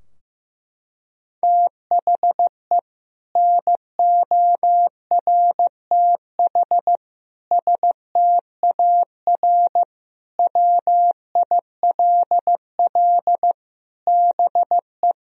Morse code, heard as a keyed tone, about 15 wpm, 700 Hz. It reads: THE NORTH STAR WILL BE
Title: Sentences from Top 1000 Words - Review Part 3 - 15wpm